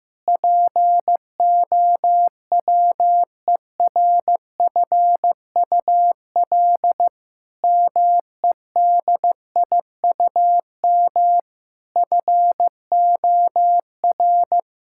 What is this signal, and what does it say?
Morse code, heard as a keyed tone, about 15 wpm, 700 Hz. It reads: POWERFUL MEDIUM FOR
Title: Sentences from Top 1000 Words - Review Part 6 - 15wpm